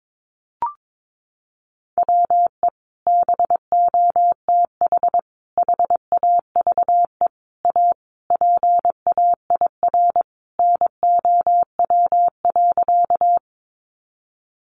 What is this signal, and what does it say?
Morse code, heard as a keyed tone, about 22 wpm, 700 Hz. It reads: WE BOTH HAVE A PAIR NOW.
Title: Sentences from Top 1200 Words 22wpm